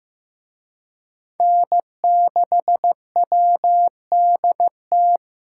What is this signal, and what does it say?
Morse code, heard as a keyed tone, about 18 wpm, 700 Hz. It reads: N6WDT
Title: Speed Racing - USA 1x3 Call Signs 15wpm